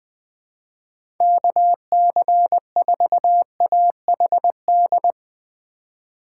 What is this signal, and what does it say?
Morse code, heard as a keyed tone, about 20 wpm, 700 Hz. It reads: KC4AHD
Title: USA Call Signs - 6 Characters - 2x Character Spacing - 20wpm